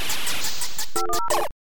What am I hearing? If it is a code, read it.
DTMF, heard as telephone touch tones, tones of about 110 ms. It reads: 510
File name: kandi-hippie kandi-hippie---dial-up-glitchtune